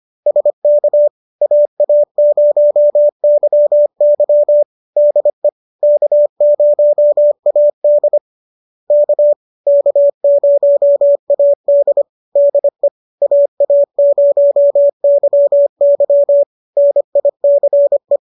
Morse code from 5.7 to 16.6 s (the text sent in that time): K0AD K K0AD DE AA0YY